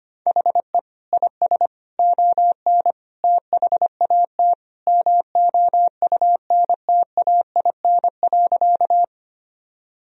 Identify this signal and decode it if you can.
Morse code, heard as a keyed tone, about 25 wpm, 700 Hz. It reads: HE IS ON THAT MOUNTAIN.
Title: Sentences from Top 1000 Words - Review Part 1 - 25wpm